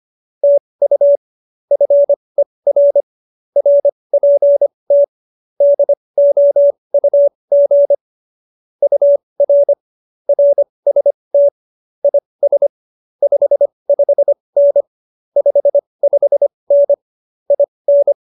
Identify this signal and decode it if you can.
Morse code, heard as a keyed tone, about 25 wpm, spaced as for 20 wpm, 570 Hz. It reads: TU FER RPT DOUG UR RST IS 55N 55N IN